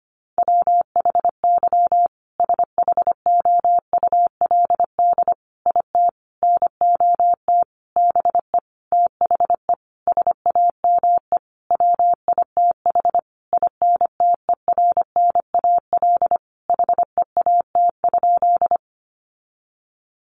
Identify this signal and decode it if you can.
Morse code, heard as a keyed tone, about 25 wpm, 700 Hz. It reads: WHY SHOULD IT NOT BE THE SAME WITH INTERNAL HEAT?
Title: Sentences from Top 1400 Words 25wpm